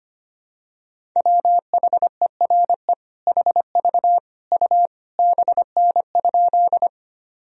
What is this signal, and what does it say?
Morse code, heard as a keyed tone, about 25 wpm, 700 Hz. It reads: WHERE HV U BN?
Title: Sentences from Top 200 Words and Abbreviations 25wpm